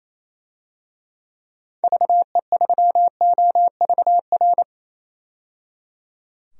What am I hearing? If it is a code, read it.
Morse code, heard as a keyed tone, about 28 wpm, 700 Hz. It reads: VE3OVR
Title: DXCC-100 Call Signs - 6 Characters - 28wpm